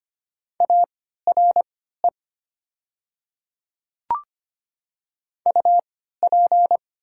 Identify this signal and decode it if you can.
Morse code, heard as a keyed tone, about 25 wpm, spaced as for 14 wpm, 700 Hz. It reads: ARE UP
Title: 3x Character Spacing - Top 100 Words - 25wpm